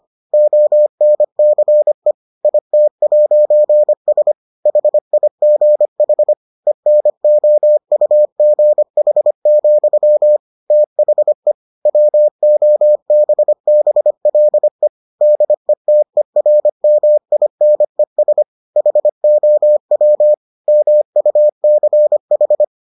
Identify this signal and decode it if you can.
Morse code, heard as a keyed tone, about 25 wpm, 600 Hz. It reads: ONCE IT'S HIGH ENOUGH, THE WOBBLE DETERMINES HOW MUCH